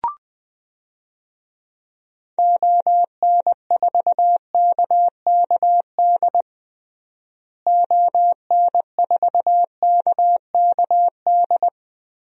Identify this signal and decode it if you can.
Morse code, heard as a keyed tone, about 21 wpm, 700 Hz. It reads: ON4KKD ON4KKD
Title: DXCC-100 Call Signs - 6 Characters - Repeated 2x - 20wpm